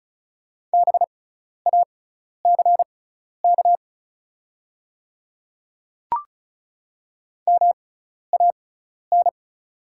Morse code, heard as a keyed tone, about 36 wpm, spaced as for 12 wpm, 700 Hz. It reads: BACK MAN
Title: 6x Character Spacing - Top 200 Words - 35wpm